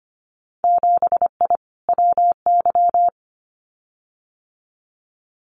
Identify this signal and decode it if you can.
Morse code, heard as a keyed tone, about 25 wpm, 700 Hz. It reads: 7I WY
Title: Winter Field Day Exchanges Encore 25wpm